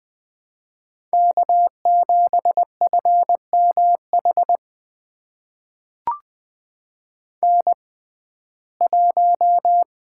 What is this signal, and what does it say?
Morse code, heard as a keyed tone, about 21 wpm, spaced as for 15 wpm, 700 Hz. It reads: K7FMH N1
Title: USA Call Signs - 5 Characters - 6x Character Spacing - 20wpm